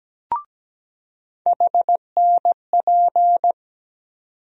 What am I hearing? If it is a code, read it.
Morse code, heard as a keyed tone, about 17 wpm, 700 Hz. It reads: HNP
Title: USA Call Sign Postfixes - 3 Letters 17wpm